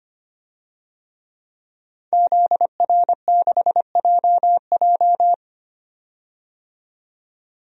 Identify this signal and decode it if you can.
Morse code, heard as a keyed tone, about 25 wpm, 700 Hz. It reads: ZR6JJ
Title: DXCC-100 Call Signs - 5 Characters - 25wpm